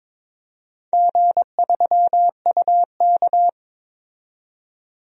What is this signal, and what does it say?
Morse code, heard as a keyed tone, about 22 wpm, 700 Hz. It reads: G3UK
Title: DXCC-100 Call Signs - 4 Characters - 22wpm